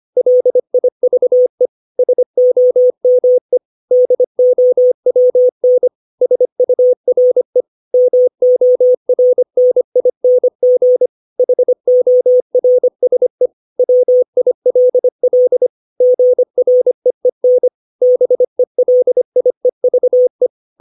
Morse code, heard as a keyed tone, about 25 wpm, 500 Hz. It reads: LIVE SOME DOWN SURE MORNING HORSE WILL GREEN BELIEVE
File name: 632 words / 632 Words 25WPM